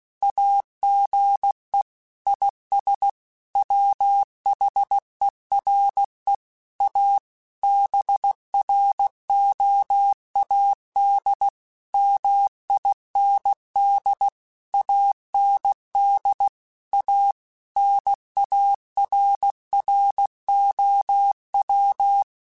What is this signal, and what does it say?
Morse code, heard as a keyed tone, about 16 wpm, 800 Hz. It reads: AGE IS WHERE A BROAD MIND AND A NARROW